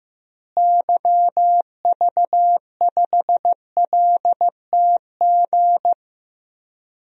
Morse code, heard as a keyed tone, about 15 wpm, 700 Hz. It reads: YV5LTG
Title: DXCC-100 Call Signs - 6 Characters - Repeated 3x - 15wpm